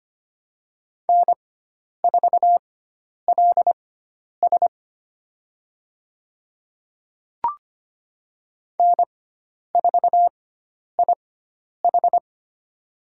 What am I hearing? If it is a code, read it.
Morse code, heard as a keyed tone, about 25 wpm, spaced as for 10 wpm, 700 Hz. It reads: N4LS N4IH
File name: Lesson 16g - Letter c - Call Signs 25wpm